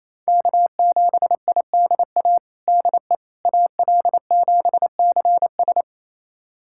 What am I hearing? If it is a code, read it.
Morse code, heard as a keyed tone, about 28 wpm, 700 Hz. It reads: K7IDA DE AL7CS
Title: USA Callsign DE Callsign - 5 Characters - 28wpm